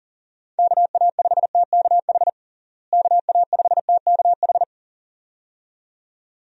Morse code, heard as a keyed tone, about 40 wpm, 700 Hz. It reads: KA5TKH KA5TKH
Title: USA Call Signs - 6 Characters - Repeated 2x - 40wpm